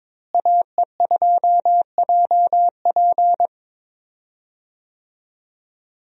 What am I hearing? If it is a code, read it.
Morse code, heard as a keyed tone, about 21 wpm, 700 Hz. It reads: AE2JP
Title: USA Call Signs - 5 Characters - Repeated 3x - 22wpm